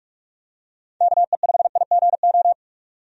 Morse code, heard as a keyed tone, about 42 wpm, 700 Hz. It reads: KE5IGO
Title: USA Call Signs - 6 Characters - 45wpm